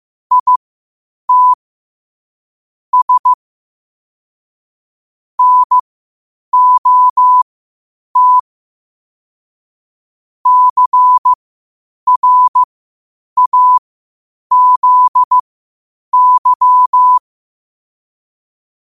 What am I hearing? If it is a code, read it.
Morse code, heard as a keyed tone, about 15 wpm, spaced as for 8 wpm, 1000 Hz. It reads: IT S NOT CRAZY
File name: qotdfiles / 8wpm